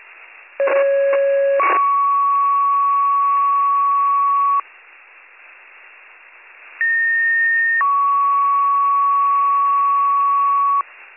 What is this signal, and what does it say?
Two-tone page, heard as a one-second tone, then a three-second tone, about 550 then 1120 Hz, about 1850 then 1120 Hz. It is heard from 0.6 to 10.8 s.